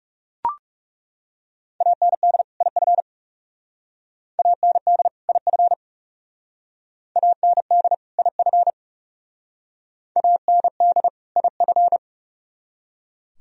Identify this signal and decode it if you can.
Morse code, heard as a keyed tone, about 36 wpm, 700 Hz. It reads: AND IF AND IF AND IF AND IF
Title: Speed-Racing - Sets of 2 Words 30wpm